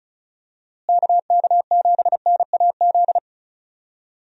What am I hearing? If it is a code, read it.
Morse code, heard as a keyed tone, about 34 wpm, 700 Hz. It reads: KK7NAZ
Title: USA Call Signs - 6 Characters - Repeated 3x - 35wpm